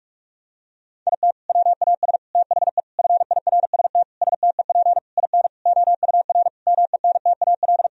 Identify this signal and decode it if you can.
Morse code, heard as a keyed tone, about 46 wpm, 700 Hz. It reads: IT WAS THE FIRST STEP IN OUR MENTAL
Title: Sentences from Top 1000 Words - Review Part 4 - 45wpm